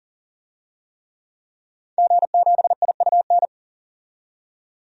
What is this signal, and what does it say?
Morse code, heard as a keyed tone, about 40 wpm, 700 Hz. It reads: G7IUN
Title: DXCC-100 Call Signs - 5 Characters - 40wpm